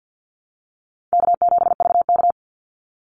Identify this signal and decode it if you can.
Morse code, heard as a keyed tone, about 50 wpm, 700 Hz. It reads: K7VK
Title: NA SOTA Chaser Callsigns 50wpm